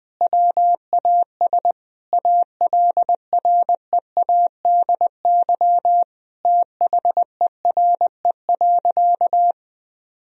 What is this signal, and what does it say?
Morse code, heard as a keyed tone, about 20 wpm, 700 Hz. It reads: WAS ALREADY THERE.
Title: Sentences from Top 1000 Words - Review Part 1 - 20wpm